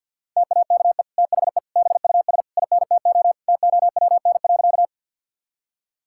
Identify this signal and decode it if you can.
Morse code, heard as a keyed tone, about 50 wpm, 700 Hz. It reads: TAKE THE BUS INTO TOWN.
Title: Sentences from Top 1000 Words - Review Part 6 - 50wpm